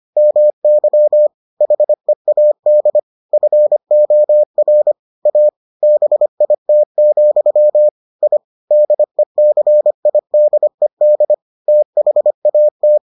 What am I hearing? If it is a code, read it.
Morse code, heard as a keyed tone, about 25 wpm, 600 Hz. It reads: MY HEAD FOR A BIT, I DECIDED THAT